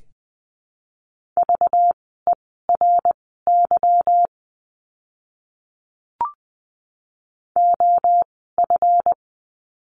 Morse code, heard as a keyed tone, about 20 wpm, spaced as for 14 wpm, 700 Hz. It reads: VERY OF
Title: 192 Lesson 31f - Letter v - Words 20wpm